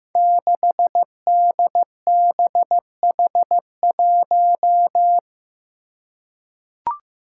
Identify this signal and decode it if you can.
Morse code, heard as a keyed tone, about 15 wpm, 700 Hz. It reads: 6DBH1
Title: Five Challenging Letter-Number Combinations 15wpm